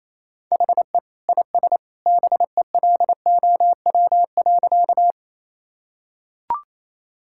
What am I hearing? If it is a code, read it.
Morse code, heard as a keyed tone, about 28 wpm, 700 Hz. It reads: HE IS BELOW.
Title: Sentences from Top 1000 Words - Review Part 5 - 28wpm